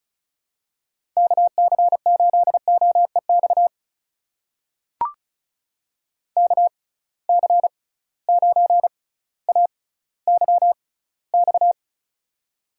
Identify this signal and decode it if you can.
Morse code, heard as a keyed tone, about 36 wpm, spaced as for 18 wpm, 700 Hz. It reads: KC8OEX KC9AYX
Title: USA Call Signs - 6 Characters - 6x Character Spacing - 35wpm